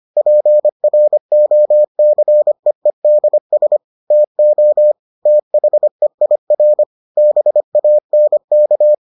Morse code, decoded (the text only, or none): PROCEEDS TO THEIR BANK